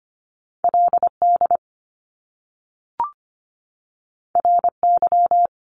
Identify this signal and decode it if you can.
Morse code, heard as a keyed tone, about 25 wpm, 700 Hz. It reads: LD RY
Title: Top Two Letter Combinations in Words 25wpm